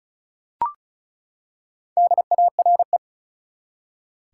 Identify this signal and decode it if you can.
Morse code, heard as a keyed tone, about 36 wpm, 700 Hz. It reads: DARE
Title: Four Letter Words - 35wpm